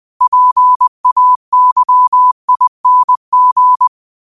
Morse code, heard as a keyed tone, about 20 wpm, 1000 Hz. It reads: PAYING